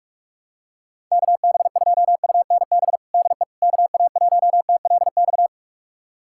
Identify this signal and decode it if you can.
Morse code, heard as a keyed tone, about 46 wpm, 700 Hz. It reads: KB2UNB DE KA1TLX
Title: USA Callsign DE Callsign - 6 Characters - 45wpm